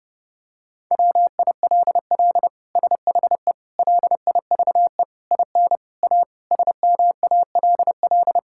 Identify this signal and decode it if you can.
Morse code, heard as a keyed tone, about 32 wpm, 700 Hz. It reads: WILL SHE LIVE IN A SMALL
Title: Sentences from Top 1000 Words - Review Part 6 - 30wpm